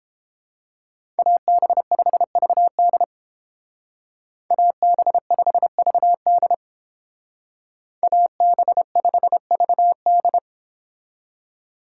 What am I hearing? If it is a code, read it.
Morse code, heard as a keyed tone, about 29 wpm, 700 Hz. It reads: AB5VD AB5VD AB5VD